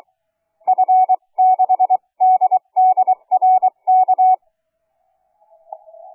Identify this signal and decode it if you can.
Morse code, heard as a keyed tone, about 23 wpm, spaced as for 20 wpm, 750 Hz. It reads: F6DDRK
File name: cw forum / KP2-K3CT 1827 le 24-02-2009 04h43